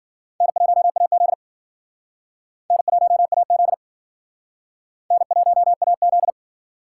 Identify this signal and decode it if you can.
Morse code, heard as a keyed tone, about 50 wpm, 700 Hz. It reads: N1AZ N1AZ N1AZ
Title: Speed Racing - USA 1x2 Call Signs 40wpm